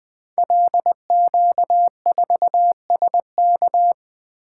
Morse code, heard as a keyed tone, about 20 wpm, 700 Hz. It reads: LQ4SK
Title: Code Groups - 5 Letter:Numbers - 20wpm